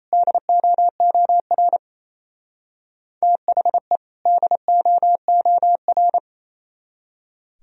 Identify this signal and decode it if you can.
Morse code, heard as a keyed tone, about 33 wpm, 700 Hz. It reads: DOOR THE DOOR
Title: Speed-Racing - Sets of 2 Words 28wpm